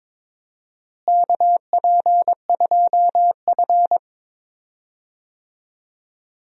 Morse code, heard as a keyed tone, about 21 wpm, 700 Hz. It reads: KP2F
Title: DXCC-100 Call Signs - 4 Characters - Repeated 2x - 22wpm